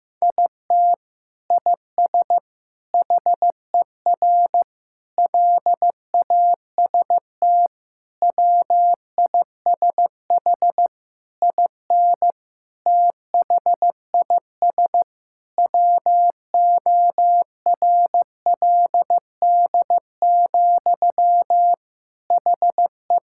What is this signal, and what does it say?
Morse code, heard as a keyed tone, about 15 wpm, 700 Hz. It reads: IT IS HER LAST WISH IN THIS WORLD, HE
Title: Sentences from Top 1600 Words 15wpm